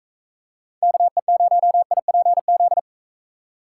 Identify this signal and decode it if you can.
Morse code, heard as a keyed tone, about 39 wpm, 700 Hz. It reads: KE0IPZ